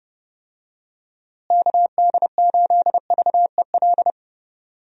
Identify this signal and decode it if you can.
Morse code, heard as a keyed tone, about 32 wpm, 700 Hz. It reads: KD8VEL
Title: USA Call Signs - 6 Characters - Repeated 3x - 30wpm